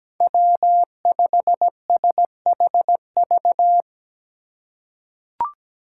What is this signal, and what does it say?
Morse code, heard as a keyed tone, about 17 wpm, 700 Hz. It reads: W5SHV
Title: USA Call Signs - 5 Characters - 2x Character Spacing - 17wpm